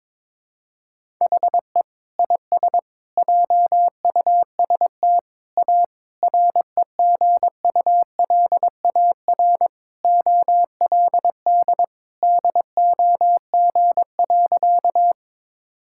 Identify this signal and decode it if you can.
Morse code, heard as a keyed tone, about 22 wpm, 700 Hz. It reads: HE IS JUST A REGULAR OLD DOG.
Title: Sentences from Top 1000 Words - Review Part 6 - 22wpm